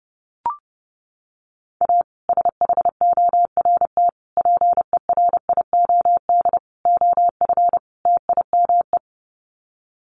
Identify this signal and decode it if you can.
Morse code, heard as a keyed tone, about 30 wpm, 700 Hz. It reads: A SHORT PERIOD OF TIME
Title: Sets of 5 Words Encore 30wpm